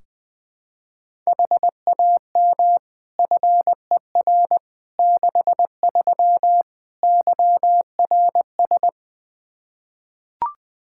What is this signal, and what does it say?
Morse code, heard as a keyed tone, about 20 wpm, 700 Hz. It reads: HAM FER 63 YRS
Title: Practice QSO - Exchange 3 - More Info - 20wpm